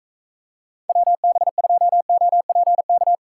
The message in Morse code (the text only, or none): WB2OPK